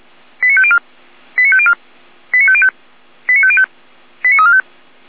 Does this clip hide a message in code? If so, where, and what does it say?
Yes: five-tone selective call (ZVEI) from 0.4 to 4.6 s, heard as quick runs of five tones, tones of about 70 ms. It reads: 89504 89504 89505 89505 89345